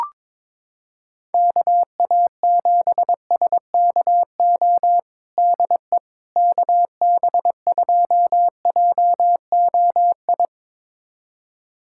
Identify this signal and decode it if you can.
Morse code, heard as a keyed tone, about 22 wpm, 700 Hz. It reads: KA7SKO DE KB2JOI
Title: USA Callsign DE Callsign - 6 Characters - 22wpm